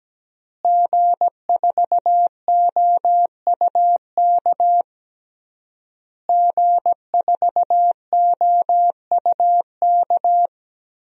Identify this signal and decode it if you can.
Morse code, heard as a keyed tone, about 17 wpm, 700 Hz. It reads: G4OUK G4OUK
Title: DXCC-100 Call Signs - 5 Characters - Repeated 3x - 17wpm